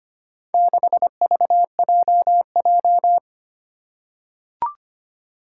Morse code, heard as a keyed tone, about 25 wpm, 700 Hz. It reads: 6VJJ
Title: Four Challenging Letter-Number Combinations 25wpm